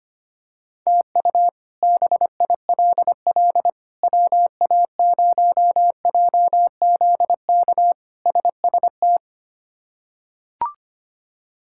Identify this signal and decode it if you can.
Morse code, heard as a keyed tone, about 25 wpm, 700 Hz. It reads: TU BILL WA0JZK SST
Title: K1USN SST QSO Practice - 25wpm